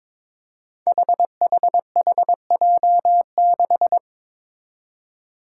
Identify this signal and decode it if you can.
Morse code, heard as a keyed tone, about 22 wpm, 700 Hz. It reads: HHHJ6
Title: Five Challenging Letter-Number Combinations 22wpm